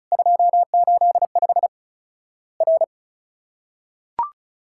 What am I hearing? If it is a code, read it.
Morse code, heard as a keyed tone, about 34 wpm, 700 Hz. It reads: 285 R
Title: Worked All Europe DX Contest - QTC Exchange - 35wpm